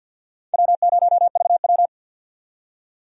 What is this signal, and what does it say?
Morse code, heard as a keyed tone, about 50 wpm, 700 Hz. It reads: W0VW
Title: USA Call Signs - 4 Characters - 50wpm